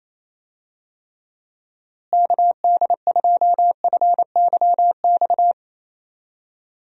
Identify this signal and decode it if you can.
Morse code, heard as a keyed tone, about 28 wpm, 700 Hz. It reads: KD2FYX
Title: USA Call Signs - 6 Characters - Repeated 2x - 28wpm